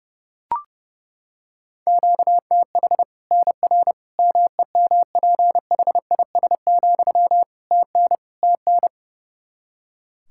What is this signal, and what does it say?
Morse code, heard as a keyed tone, about 30 wpm, 700 Hz. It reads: QTH NR MEMPHIS, TN TN
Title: Practice QSO - Exchange 1 - Beginning - 30wpm